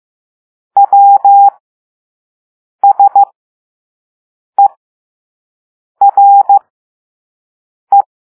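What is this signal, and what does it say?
Morse code, heard as a keyed tone, about 15 wpm, spaced as for 5 wpm, 800 Hz. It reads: WSERE
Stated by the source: Morse code lesson 5 drill of random five-letter groups using all letters learned so far.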